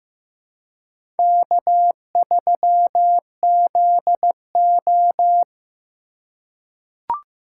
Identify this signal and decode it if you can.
Morse code, heard as a keyed tone, about 15 wpm, 700 Hz. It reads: K3ZO
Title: USA Call Signs - 4 Characters - 3x Character Spacing - 15wpm